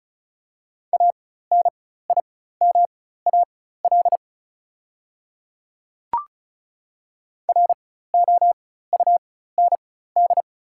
Morse code, heard as a keyed tone, about 36 wpm, spaced as for 16 wpm, 700 Hz. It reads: ANIMAL ROUND
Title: 4x Character Spacing - Top 200 Words - 35wpm